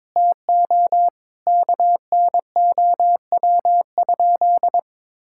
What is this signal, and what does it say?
Morse code, heard as a keyed tone, about 22 wpm, 700 Hz. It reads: TO KNOW?
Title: Sentences from Top 1600 Words 22wpm